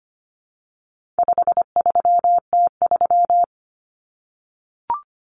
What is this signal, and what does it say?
Morse code, heard as a keyed tone, about 25 wpm, 700 Hz. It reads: 53T3
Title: Common Cut Numbers - 4-Digit 25wpm